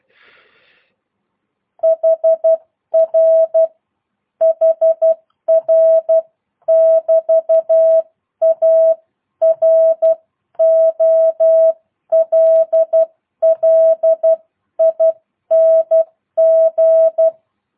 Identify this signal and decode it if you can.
Morse code, heard as a keyed tone, about 12 wpm, 660 Hz. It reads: HR HR <BT> A ROLLING